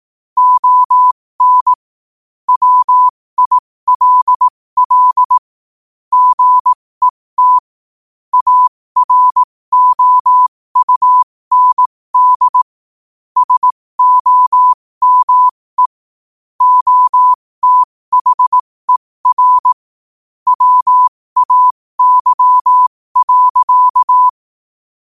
Morse code, heard as a keyed tone, about 18 wpm, spaced as for 15 wpm, 1000 Hz. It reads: ON WILL GET AROUND SOME OTHER WAY.
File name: qotdfiles / 15wpm